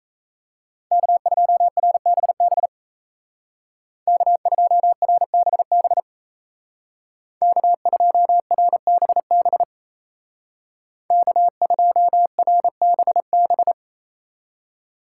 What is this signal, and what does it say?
Morse code, heard as a keyed tone, about 36 wpm, 700 Hz. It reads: K2RBB K2RBB K2RBB K2RBB